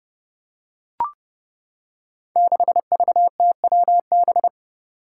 Morse code, heard as a keyed tone, about 30 wpm, 700 Hz. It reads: 6VTWB
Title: Five Challenging Letter-Number Combinations 30wpm